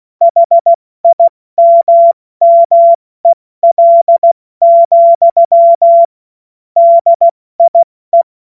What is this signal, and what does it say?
Morse code, heard as a keyed tone, about 16 wpm, 680 Hz. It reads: HIMMEL, DIE
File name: Morsekurs / Witze / Witz 17.Oktober 2023 70 F80